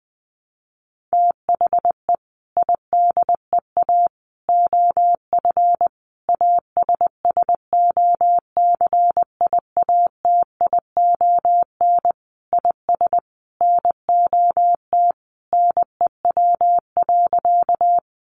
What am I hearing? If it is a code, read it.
Morse code, heard as a keyed tone, about 20 wpm, 700 Hz. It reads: THE IDEA OF ASSOCIATION IS NOT NEW.
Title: Sentences from Top 1100 Words 20wpm